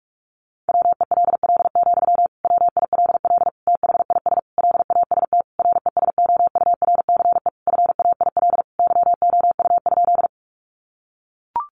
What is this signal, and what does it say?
Morse code, heard as a keyed tone, about 44 wpm, 700 Hz. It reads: WELL, WILL THIS LAST RESOURCE FAIL YOU?
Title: Sentences from Top 1300 Words 45wpm